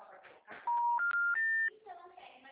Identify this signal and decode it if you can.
Special information tone, with speech.
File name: out- -2012- - - .21638